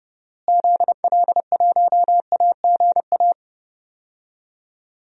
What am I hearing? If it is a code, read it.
Morse code, heard as a keyed tone, about 30 wpm, 700 Hz. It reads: ZL1AGA
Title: DXCC-100 Call Signs - 6 Characters - Repeated 3x - 30wpm